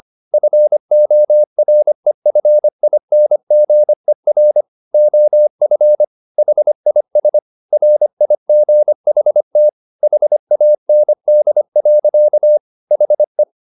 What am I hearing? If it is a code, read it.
Morse code, heard as a keyed tone, about 25 wpm, 600 Hz. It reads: FOREFINGER OF HIS RIGHT HAND. HE
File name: the old man and the sea part 1 25wpm